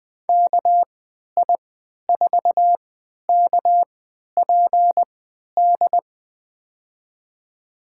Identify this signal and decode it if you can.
Morse code, heard as a keyed tone, about 21 wpm, spaced as for 12 wpm, 700 Hz. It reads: KI4KPD
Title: USA Call Signs - 6 Characters - 3x Character Spacing - 20wpm